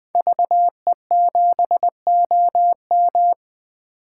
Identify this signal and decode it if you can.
Morse code, heard as a keyed tone, about 21 wpm, 700 Hz. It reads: VE7OM
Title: 2023 SOTA Chaser Call Signs - 20wpm